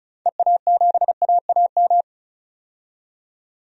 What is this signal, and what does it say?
Morse code, heard as a keyed tone, about 36 wpm, 700 Hz. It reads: EA7AAM
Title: DXCC-100 Call Signs - 6 Characters - 35wpm